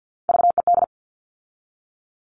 Morse code, heard as a keyed tone, about 50 wpm, 700 Hz. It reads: VED
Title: All Cut Numbers - 3-Digit 50wpm